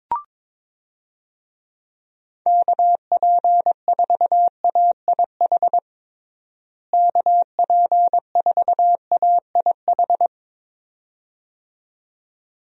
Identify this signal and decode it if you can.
Morse code, heard as a keyed tone, about 22 wpm, 700 Hz. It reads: KP4AIH KP4AIH
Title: DXCC-100 Call Signs - 6 Characters - Repeated 2x - 22wpm